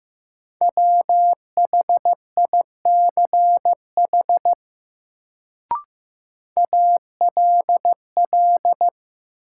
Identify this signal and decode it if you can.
Morse code, heard as a keyed tone, about 15 wpm, 700 Hz. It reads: WHICH ALL
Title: Sets of 2 Words - New Word Drill - 15wpm